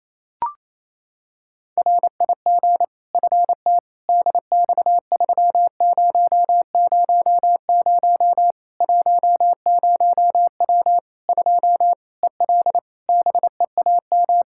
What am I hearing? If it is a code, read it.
Morse code, heard as a keyed tone, about 27 wpm, 700 Hz. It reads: RIG FT DX3000 10W 2 EL BEAM
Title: Practice QSO - Exchange 2 - Info - 28wpm